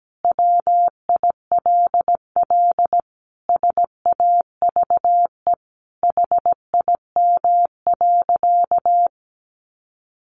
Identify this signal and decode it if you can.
Morse code, heard as a keyed tone, about 17 wpm, 700 Hz. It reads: WILL SAVE HIM.
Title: Sentences from Top 1100 Words 17wpm